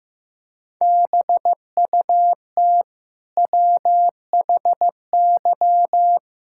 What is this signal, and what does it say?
Morse code, heard as a keyed tone, about 14 wpm, 700 Hz. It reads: BUT WHY
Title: Sentences from Top 1000 Words - Review Part 6 - 15wpm